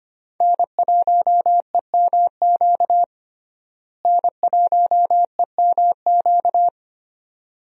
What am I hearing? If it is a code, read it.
Morse code, heard as a keyed tone, about 25 wpm, 700 Hz. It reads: N1EMQ N1EMQ
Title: USA Call Signs - 5 Characters - Repeated 2x - 25wpm